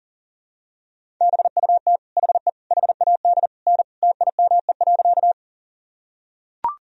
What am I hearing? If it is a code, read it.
Morse code, heard as a keyed tone, about 40 wpm, 700 Hz. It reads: BUT HE HAD N TIME.
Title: Sentences from Top 100 Words and Abbreviations 40wpm